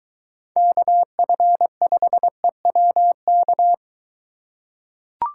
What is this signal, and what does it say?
Morse code, heard as a keyed tone, about 23 wpm, 700 Hz. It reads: KF5EWK